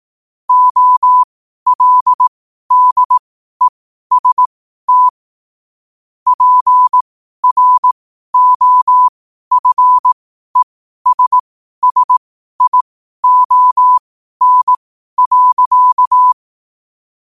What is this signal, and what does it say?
Morse code, heard as a keyed tone, about 18 wpm, spaced as for 12 wpm, 1000 Hz. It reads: OLDEST PROFESSION.
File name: qotdfiles / 12wpm